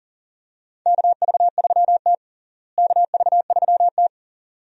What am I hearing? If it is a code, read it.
Morse code, heard as a keyed tone, about 40 wpm, 700 Hz. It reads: KV3T KV3T
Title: USA Call Signs - 4 Characters - Repeated 2x - 40wpm